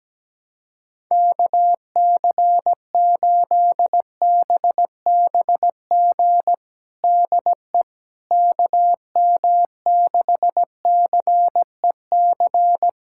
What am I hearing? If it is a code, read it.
Morse code, heard as a keyed tone, about 17 wpm, 700 Hz. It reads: KC8BBG DE KM6CEC
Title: USA Callsign DE Callsign 17wpm